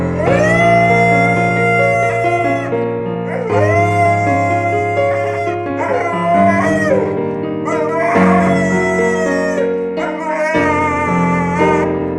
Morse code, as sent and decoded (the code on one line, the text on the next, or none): none